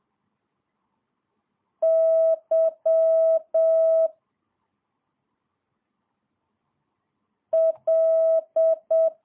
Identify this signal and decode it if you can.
Morse code, heard as a keyed tone, about 7 wpm, 660 Hz. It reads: Y L
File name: jh8cbh / 2022cw